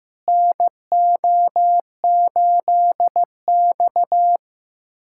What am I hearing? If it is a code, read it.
Morse code, heard as a keyed tone, about 15 wpm, 700 Hz. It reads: NO8X
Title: USA Callsign DE Callsign - 4 Characters - 15wpm